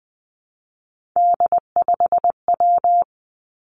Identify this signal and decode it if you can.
Morse code, heard as a keyed tone, about 20 wpm, 700 Hz. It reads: D5W
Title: Three Challenging Letter-Number Combinations 20wpm